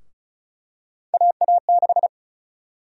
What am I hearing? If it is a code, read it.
Morse code, heard as a keyed tone, about 36 wpm, 700 Hz. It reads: AA6
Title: USA Call Sign Prefixes - Repeated 2x - 35wpm